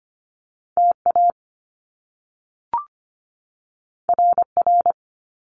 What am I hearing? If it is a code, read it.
Morse code, heard as a keyed tone, about 25 wpm, 700 Hz. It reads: TA RR
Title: Top Two Letter Combinations in Words 25wpm